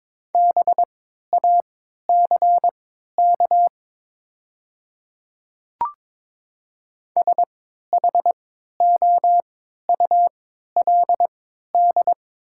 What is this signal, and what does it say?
Morse code, heard as a keyed tone, about 22 wpm, spaced as for 12 wpm, 700 Hz. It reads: BACK SHOULD
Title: 3x Character Spacing - Top 500 Words - 22wpm